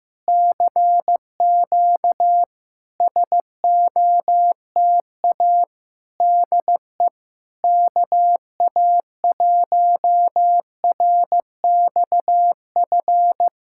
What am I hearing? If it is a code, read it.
Morse code, heard as a keyed tone, about 15 wpm, 700 Hz. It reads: CQ SOTA DE KA1RXF
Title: SOTA QSO Practice 15wpm